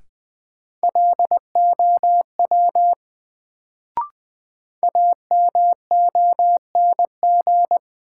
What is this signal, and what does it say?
Morse code, heard as a keyed tone, about 20 wpm, 700 Hz. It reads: LOW AMONG
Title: Sentences from Top 500 Words - New Word Drill - 20wpm